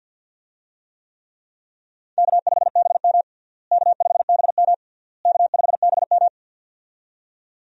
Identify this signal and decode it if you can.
Morse code, heard as a keyed tone, about 46 wpm, 700 Hz. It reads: K5BM K5BM K5BM
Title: USA Call Signs - 4 Characters - Repeated 3x - 50wpm